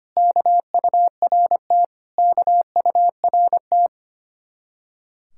Morse code, heard as a keyed tone, about 32 wpm, 700 Hz. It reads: KURT KURT